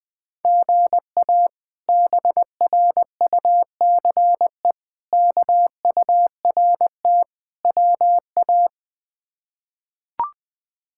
Morse code, heard as a keyed tone, about 20 wpm, 700 Hz. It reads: GA BRUCE KURT WA
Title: K1USN SST QSO Practice - 20wpm